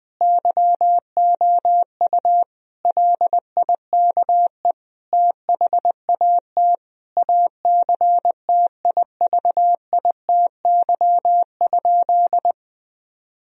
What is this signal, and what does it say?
Morse code, heard as a keyed tone, about 21 wpm, 700 Hz. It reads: YOU LIKE THAT ACTIVITY?
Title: Sentences from Top 1000 Words - Review Part 4 - 20wpm